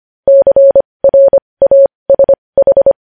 Morse code, heard as a keyed tone, about 25 wpm, spaced as for 20 wpm, 550 Hz. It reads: CRASH
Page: news-headlines-in-morse-code-at-20-wpm